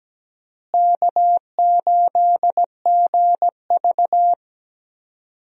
Morse code, heard as a keyed tone, about 17 wpm, 700 Hz. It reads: K8GV
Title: USA Call Signs - 4 Characters - Repeated 2x - 17wpm